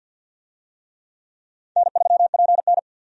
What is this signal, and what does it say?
Morse code, heard as a keyed tone, about 52 wpm, 700 Hz. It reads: N3PN
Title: USA Call Signs - 4 Characters - 50wpm